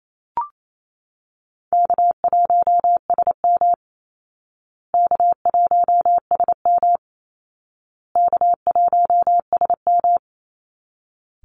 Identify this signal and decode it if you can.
Morse code, heard as a keyed tone, about 28 wpm, 700 Hz. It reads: K1SM K1SM K1SM
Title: Easy Call Signs Repeated 3x - 28wpm